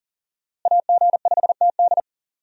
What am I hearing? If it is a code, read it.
Morse code, heard as a keyed tone, about 37 wpm, 700 Hz. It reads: AG5TD
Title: USA Call Signs - 5 Characters - Repeated 2x - 40wpm